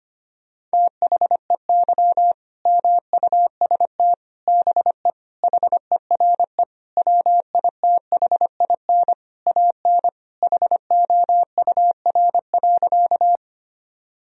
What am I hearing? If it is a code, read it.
Morse code, heard as a keyed tone, about 25 wpm, 700 Hz. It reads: THEY MUST BE HERE WITHIN AN HOUR.
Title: Sentences from Top 1000 Words - Review Part 4 - 25wpm